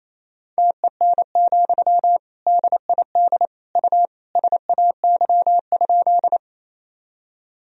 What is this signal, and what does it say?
Morse code, heard as a keyed tone, about 28 wpm, 700 Hz. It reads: TEN, DID U SAY?
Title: Sentences from Top 400 Words and Abbreviations 28wpm